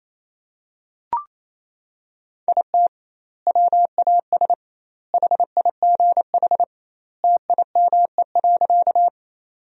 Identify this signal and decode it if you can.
Morse code, heard as a keyed tone, about 27 wpm, 700 Hz. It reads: IT WAS HIGH TIME.
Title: 2x Word-Spacing - Sentences from Top 200 Words - 28wpm